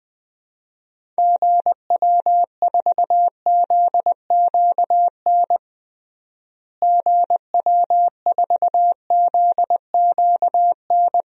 Morse code, heard as a keyed tone, about 21 wpm, 700 Hz. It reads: GW4ZQN GW4ZQN
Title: DXCC-100 Call Signs - 6 Characters - Repeated 3x - 20wpm